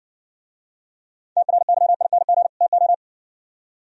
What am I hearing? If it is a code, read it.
Morse code, heard as a keyed tone, about 62 wpm, 690 Hz. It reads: TRYING TO